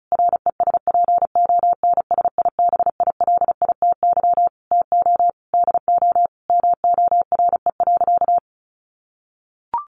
Morse code, heard as a keyed tone, about 34 wpm, 700 Hz. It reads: RESPONSIBILITY TO DO MORE.
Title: Sentences from Top 1400 Words 35wpm